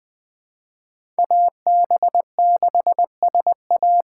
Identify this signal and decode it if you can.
Morse code, heard as a keyed tone, about 20 wpm, 700 Hz. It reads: AB6SA
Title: USA Call Signs - 5 Characters - 4x Character Spacing - 20wpm